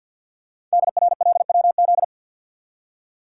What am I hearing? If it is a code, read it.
Morse code, heard as a keyed tone, about 48 wpm, 700 Hz. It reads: DRLWZ
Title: Code Groups - 5 Letters - 50wpm